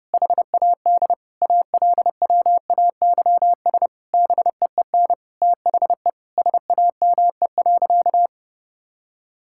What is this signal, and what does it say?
Morse code, heard as a keyed tone, about 29 wpm, 700 Hz. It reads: HAD ALWAYS BEEN THE SAME.
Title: Sentences from Top 1000 Words - Review Part 6 - 30wpm